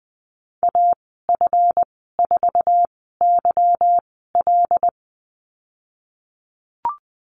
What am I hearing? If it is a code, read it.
Morse code, heard as a keyed tone, about 20 wpm, spaced as for 15 wpm, 700 Hz. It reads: AF4YL
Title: 167 Lesson 27g - Number 7 - Call Signs 20wpm